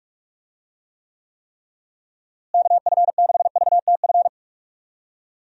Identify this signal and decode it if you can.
Morse code, heard as a keyed tone, about 46 wpm, 700 Hz. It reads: KF6VTF
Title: USA Call Signs - 6 Characters - Repeated 3x - 45wpm